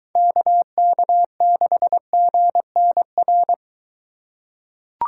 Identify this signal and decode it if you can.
Morse code, heard as a keyed tone, about 22 wpm, 700 Hz. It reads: KK6GNR